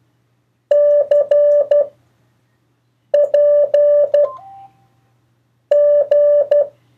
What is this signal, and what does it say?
Morse code, heard as a keyed tone, about 12 wpm, 580 Hz. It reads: C P G
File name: 1of4-cw-letters